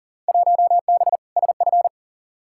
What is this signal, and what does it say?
Morse code, heard as a keyed tone, about 40 wpm, 700 Hz. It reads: 1B SF
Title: Field Day Exchanges Encore - 40wpm